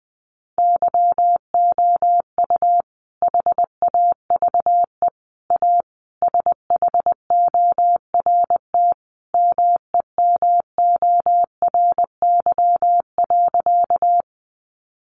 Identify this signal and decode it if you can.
Morse code, heard as a keyed tone, about 19 wpm, 700 Hz. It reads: YOU HAVE A SHORT MEMORY.
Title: Sentences from Top 1300 Words 20wpm